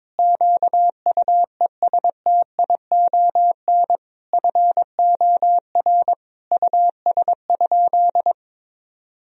Morse code, heard as a keyed tone, about 22 wpm, 700 Hz. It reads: QUESTION FOR US?
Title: Sentences from Top 1000 Words - Review Part 3 - 22wpm